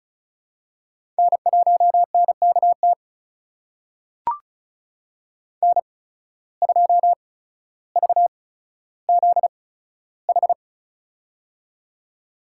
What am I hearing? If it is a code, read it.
Morse code, heard as a keyed tone, about 36 wpm, spaced as for 15 wpm, 700 Hz. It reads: N1NKT N2VZH
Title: USA Call Signs - 5 Characters - 8x Character Spacing - 35wpm